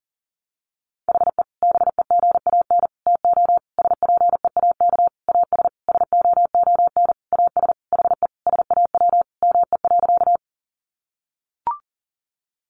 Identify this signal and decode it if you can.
Morse code, heard as a keyed tone, about 40 wpm, 700 Hz. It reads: HE BEGAN TO SPEAK AS SOON AS HE SAW ME.
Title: Sentences from Top 1100 Words 40wpm